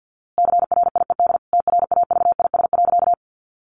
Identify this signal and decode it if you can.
Morse code, heard as a keyed tone, about 50 wpm, 700 Hz. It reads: CRIED TRAVIS.
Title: A Sound Of Thunder 50wpm